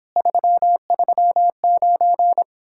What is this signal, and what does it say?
Morse code, heard as a keyed tone, about 26 wpm, 700 Hz. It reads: 339